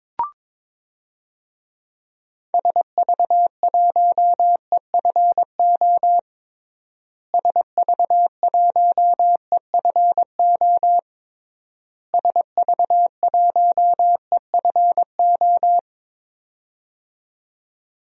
Morse code, heard as a keyed tone, about 22 wpm, 700 Hz. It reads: SV1EFO SV1EFO SV1EFO
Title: DXCC-100 Call Signs - 6 Characters - Repeated 3x - 22wpm